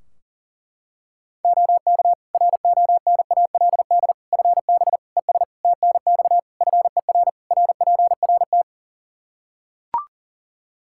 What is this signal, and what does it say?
Morse code, heard as a keyed tone, about 37 wpm, 700 Hz. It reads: OK RONALD FB ES TNX FER RPRT
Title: Practice QSO - Exchange 2 - Info - 40wpm